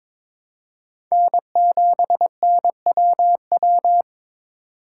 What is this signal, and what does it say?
Morse code, heard as a keyed tone, about 22 wpm, 700 Hz. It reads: N7NWW
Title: USA Call Signs - 5 Characters - 2x Character Spacing - 22wpm